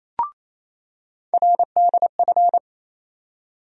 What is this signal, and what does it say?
Morse code, heard as a keyed tone, about 28 wpm, 700 Hz. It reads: RDF
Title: USA Call Sign Postfixes - 3 Letters 28wpm